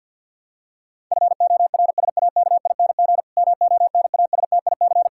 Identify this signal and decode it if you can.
Morse code, heard as a keyed tone, about 52 wpm, 700 Hz. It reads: FORSAKING MONASTIC
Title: Pangram Challenge v2 - 50wpm